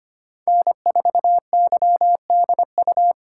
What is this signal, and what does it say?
Morse code, heard as a keyed tone, about 25 wpm, 700 Hz. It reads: N4YDU
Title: K1USN SST QSO Practice - 25wpm